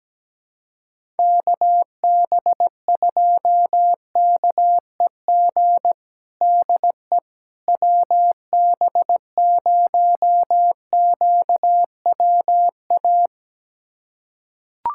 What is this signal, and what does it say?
Morse code, heard as a keyed tone, about 17 wpm, 700 Hz. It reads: KB2KEG DE WB0QWA
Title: USA Callsign DE Callsign 17wpm